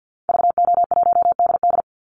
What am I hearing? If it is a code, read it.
Morse code, heard as a keyed tone, about 50 wpm, 700 Hz. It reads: VO1DD
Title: Call Signs - 50wpm